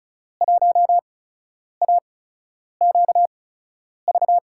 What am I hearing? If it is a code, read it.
Morse code, heard as a keyed tone, about 34 wpm, spaced as for 10 wpm, 700 Hz. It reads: 1AQV
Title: USA Call Signs - 5 Characters - 8x Character Spacing - 35wpm